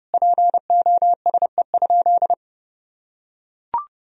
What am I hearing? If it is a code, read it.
Morse code, heard as a keyed tone, about 30 wpm, 700 Hz. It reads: POSE?
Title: Sentences from Top 1000 Words - Review Part 2 - 30wpm